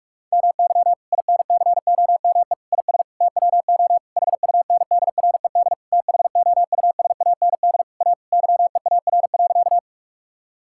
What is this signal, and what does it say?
Morse code, heard as a keyed tone, about 46 wpm, 700 Hz. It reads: MY INCOME IS TWO HUNDRED THOUSAND A YEAR.
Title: Sentences from Top 1000 Words - Review Part 5 - 45wpm